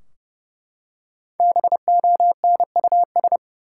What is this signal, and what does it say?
Morse code, heard as a keyed tone, about 30 wpm, 700 Hz. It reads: BONUS
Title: Five Letter Words - 30wpm